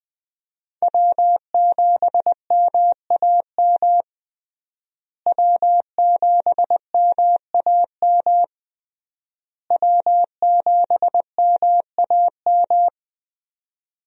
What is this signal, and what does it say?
Morse code, heard as a keyed tone, about 21 wpm, 700 Hz. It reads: W7MAM W7MAM W7MAM
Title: USA Call Signs - 5 Characters - Repeated 3x - 20wpm